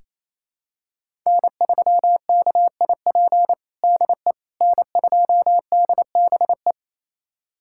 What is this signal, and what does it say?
Morse code, heard as a keyed tone, about 28 wpm, 700 Hz. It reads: N3KIP DE N2DBE
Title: USA Callsign DE Callsign - 5 Characters - 28wpm